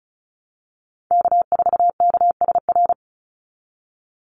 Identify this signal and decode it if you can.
Morse code, heard as a keyed tone, about 34 wpm, 700 Hz. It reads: K4KSR
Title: Call Signs Repeated 2x - 35wpm